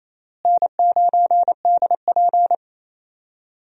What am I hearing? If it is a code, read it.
Morse code, heard as a keyed tone, about 28 wpm, 700 Hz. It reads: N9DP
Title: USA Call Signs - 4 Characters - Repeated 3x - 28wpm